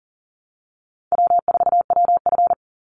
Morse code, heard as a keyed tone, about 40 wpm, 700 Hz. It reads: W4WF
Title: Call Signs - 40wpm